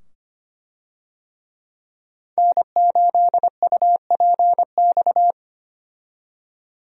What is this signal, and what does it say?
Morse code, heard as a keyed tone, about 25 wpm, 700 Hz. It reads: N8UPX
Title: USA Call Signs - 5 Characters - 25wpm